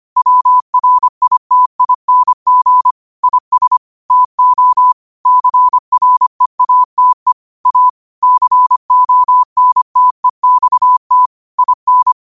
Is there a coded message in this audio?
Yes: Morse code from 0.2 to 12.1 s, heard as a keyed tone, about 25 wpm, 1000 Hz. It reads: WRITING IS TO CREATE A CONTEXT IN